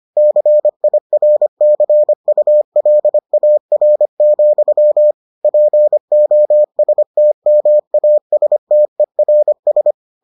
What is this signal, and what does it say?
Morse code, heard as a keyed tone, about 25 wpm, 600 Hz. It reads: CIRCULAR, POSTMASTERS